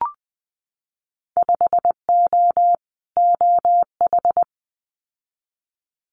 Morse code, heard as a keyed tone, about 20 wpm, 700 Hz. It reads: 5O OH
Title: Winter Field Day Exchanges Encore 20wpm